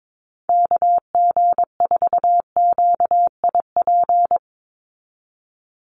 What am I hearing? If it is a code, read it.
Morse code, heard as a keyed tone, about 22 wpm, 700 Hz. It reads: KG4QIP
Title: USA Call Signs 22wpm